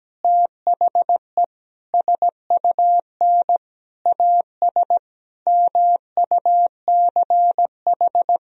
Morse code, heard as a keyed tone, about 17 wpm, 700 Hz. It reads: THE SUN AS MUCH
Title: Sentences from Top 1000 Words - Review Part 1 - 17wpm